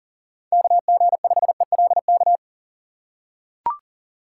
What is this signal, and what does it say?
Morse code, heard as a keyed tone, about 40 wpm, 700 Hz. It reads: KG5ELK
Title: USA Call Signs - 6x Character Spacing - 40wpm